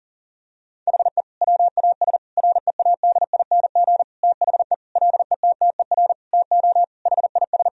Morse code, heard as a keyed tone, about 40 wpm, 700 Hz. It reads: HE WAS READING THE LETTER TO HIS
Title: Sentences from Top 1000 Words - Review Part 3 - 40wpm